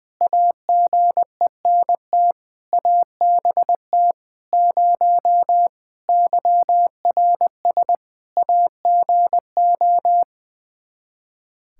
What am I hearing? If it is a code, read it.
Morse code, heard as a keyed tone, about 20 wpm, 700 Hz. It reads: AGENT ABT 0 YRS AGO
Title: Practice QSO - Exchange 3 - More Info - 20wpm